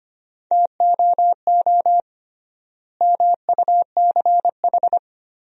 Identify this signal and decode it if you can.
Morse code, heard as a keyed tone, about 25 wpm, 700 Hz. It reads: TOO MUCH
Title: 3x Word-Spacing - Sentences from Top 200 Words - 25wpm